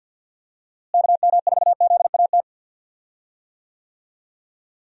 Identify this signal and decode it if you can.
Morse code, heard as a keyed tone, about 52 wpm, 700 Hz. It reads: KM4ZAT
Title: USA Call Signs - 6 Characters - Repeated 3x - 50wpm